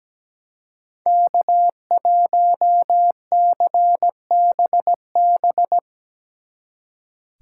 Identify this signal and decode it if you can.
Morse code, heard as a keyed tone, about 19 wpm, 700 Hz. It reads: K1CBB
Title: Speed Racing - USA 1x3 Call Signs 17wpm